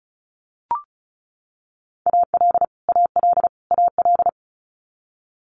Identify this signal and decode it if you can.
Morse code, heard as a keyed tone, about 36 wpm, 700 Hz. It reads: AL AL AL
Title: Multi-letter Phonemes Repeated 3x 35wpm